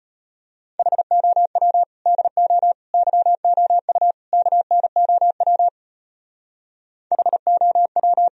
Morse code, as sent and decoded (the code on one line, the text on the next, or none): .... --- .-- / -.. --- / -.-- --- ..- / -.- -. --- .-- / .... --- .--
HOW DO YOU KNOW HOW